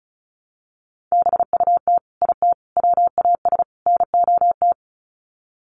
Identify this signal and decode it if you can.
Morse code, heard as a keyed tone, about 34 wpm, 700 Hz. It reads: BUT IT WAS NOT
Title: Sets of 4 Words 35wpm